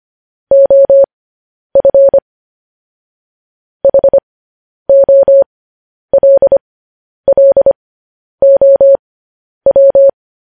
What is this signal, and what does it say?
Morse code, heard as a keyed tone, about 25 wpm, spaced as for 10 wpm, 550 Hz. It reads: OF HOLLOW
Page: news-headlines-in-morse-code-at-10-wpm